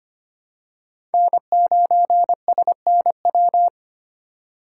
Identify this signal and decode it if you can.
Morse code, heard as a keyed tone, about 25 wpm, 700 Hz. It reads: N9SNW
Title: USA Call Signs - 5 Characters - 8x Character Spacing - 25wpm